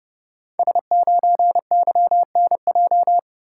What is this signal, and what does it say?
Morse code, heard as a keyed tone, about 30 wpm, 700 Hz. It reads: S9YNJ
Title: Code Groups - 5 Letter:Numbers - 30wpm